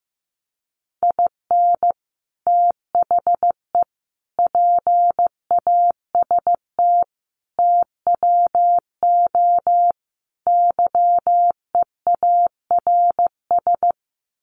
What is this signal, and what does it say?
Morse code, heard as a keyed tone, about 15 wpm, 700 Hz. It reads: IN THE PAST TWO YEARS
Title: Sets of 5 Words 15wpm